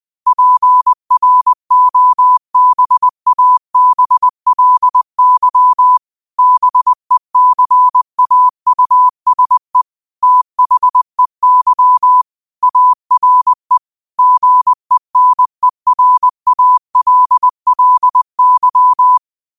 Morse code, heard as a keyed tone, about 20 wpm, 1000 Hz. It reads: PROBABLY BECAUSE THEY ARE GENERALLY
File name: qotdfiles / 20wpm